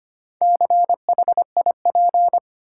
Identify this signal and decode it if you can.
Morse code, heard as a keyed tone, about 25 wpm, 700 Hz. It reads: CHIP
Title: Four Letter Words - 25wpm